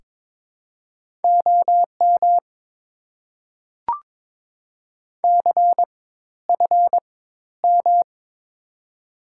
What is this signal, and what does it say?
Morse code, heard as a keyed tone, about 22 wpm, spaced as for 13 wpm, 690 Hz. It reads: OM CFM
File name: Top 90 QSO Elements - 4x Character Spacing - 22wpm